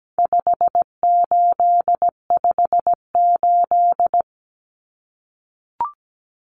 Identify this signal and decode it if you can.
Morse code, heard as a keyed tone, about 17 wpm, 700 Hz. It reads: 5858
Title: Numbers - 4-Digit 17wpm